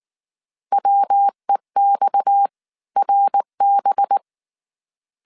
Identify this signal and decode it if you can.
Morse code, heard as a keyed tone, about 19 wpm, spaced as for 16 wpm, 800 Hz. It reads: WEXRB